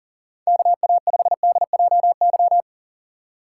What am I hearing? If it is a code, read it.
Morse code, heard as a keyed tone, about 40 wpm, 700 Hz. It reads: KA5DJY
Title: USA Call Signs - 6 Characters - 4x Character Spacing - 40wpm